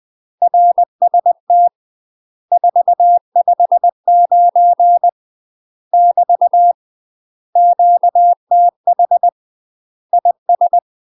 Morse code, heard as a keyed tone, about 20 wpm, 700 Hz. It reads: RST 459 <BT> QTH IS
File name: UK MASTER QSOs TRAINING / 20WPM / UK MASTER QSOs 0059